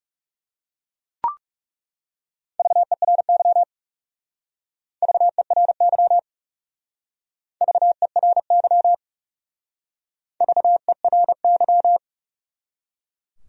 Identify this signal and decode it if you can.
Morse code, heard as a keyed tone, about 36 wpm, 700 Hz. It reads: VERY VERY VERY VERY
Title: Speed-Racing Top 500 Words 30wpm